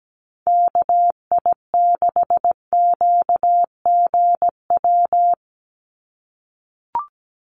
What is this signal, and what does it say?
Morse code, heard as a keyed tone, about 17 wpm, 700 Hz. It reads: KI6QGW
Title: USA Call Signs 17wpm